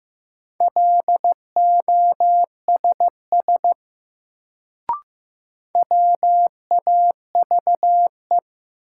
Morse code, heard as a keyed tone, about 15 wpm, 700 Hz. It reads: LOSS WAVE
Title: Sentences from Top 1400 Words - New Word Drill - 15wpm